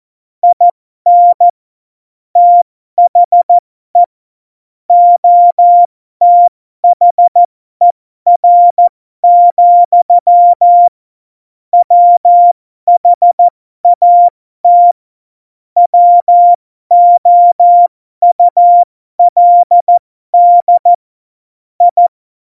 Morse code, heard as a keyed tone, about 14 wpm, spaced as for 12 wpm, 700 Hz. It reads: IN THE OTHER, WHAT WOULD I
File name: Morsekurs / Witze